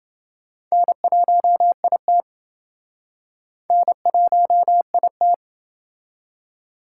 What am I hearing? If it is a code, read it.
Morse code, heard as a keyed tone, about 28 wpm, 700 Hz. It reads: N1IT N1IT